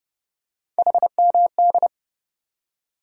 Morse code, heard as a keyed tone, about 30 wpm, 700 Hz. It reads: HMD
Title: USA Call Sign Postfixes - 3 Letters 30wpm